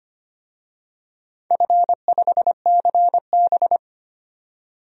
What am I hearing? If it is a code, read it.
Morse code, heard as a keyed tone, about 25 wpm, 700 Hz. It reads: F5CB
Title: DXCC-100 Call Signs - 4 Characters - 25wpm